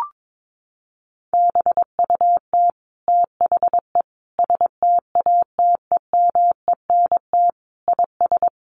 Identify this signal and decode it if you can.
Morse code, heard as a keyed tone, about 22 wpm, 700 Hz. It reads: BUT THE STATEMENT IS
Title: Sentences from Top 1200 Words 22wpm